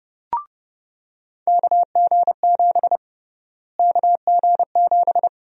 Morse code, heard as a keyed tone, about 31 wpm, 700 Hz. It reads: KG7 KG7
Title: USA Call Sign Prefixes - Repeated 2x - 30wpm